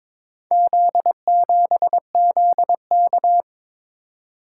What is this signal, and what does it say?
Morse code, heard as a keyed tone, about 22 wpm, 700 Hz. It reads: Z7ZK
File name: practice / 22wpm / Four Letter-Number Combinations - 22wpm